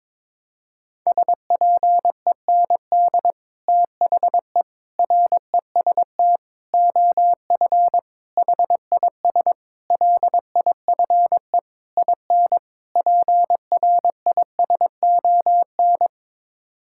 Morse code, heard as a keyed tone, about 22 wpm, 700 Hz. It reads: SPEND THE REST OF HIS LIFE IN PRISON
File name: practice / 22wpm / Sets of 8 Words - 22wpm